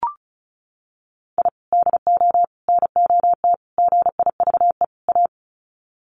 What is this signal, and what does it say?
Morse code, heard as a keyed tone, about 34 wpm, 700 Hz. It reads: I DO NOT GIVE A
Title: Sets of 5 Words Encore2 35wpm